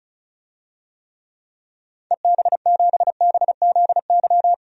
Morse code, heard as a keyed tone, about 34 wpm, 700 Hz. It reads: EB7BZY
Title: DXCC-100 Call Signs - 6 Characters - 35wpm